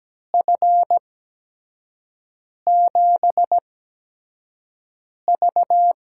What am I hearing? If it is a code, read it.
Morse code, heard as a keyed tone, about 17 wpm, spaced as for 5 wpm, 700 Hz. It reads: F7V
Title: USA Call Signs - 6 Characters - 8x Character Spacing - 17wpm